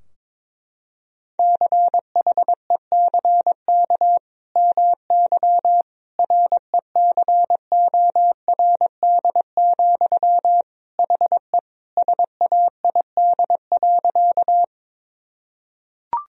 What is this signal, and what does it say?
Morse code, heard as a keyed tone, about 21 wpm, 700 Hz. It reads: CHECK MY RECORD, HE SAID.
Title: Sentences from Top 1000 Words - Review Part 3 - 22wpm